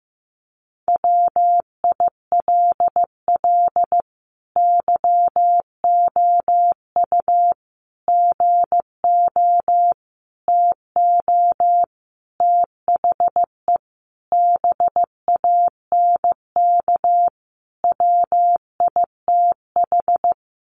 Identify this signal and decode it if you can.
Morse code, heard as a keyed tone, about 15 wpm, 700 Hz. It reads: WILL YOU GO TO THE BANK WITH
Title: Sentences from Top 900 Words 15wpm